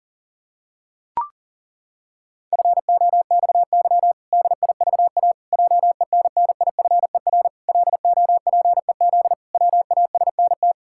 Morse code, heard as a keyed tone, about 37 wpm, 700 Hz. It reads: FOXY DIVA JENNIFER LOPEZ WASNT
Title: Pangram Challenge v2 - 40wpm